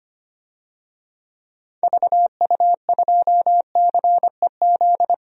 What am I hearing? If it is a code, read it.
Morse code, heard as a keyed tone, about 25 wpm, 700 Hz. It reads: VU2CEZ
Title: DXCC-100 Call Signs - 6 Characters - Repeated 2x - 25wpm